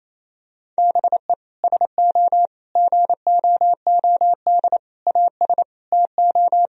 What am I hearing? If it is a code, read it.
Morse code, heard as a keyed tone, about 28 wpm, 700 Hz. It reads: BE SO GOOD AS TO
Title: Sentences from Top 1000 Words - Review Part 1 - 28wpm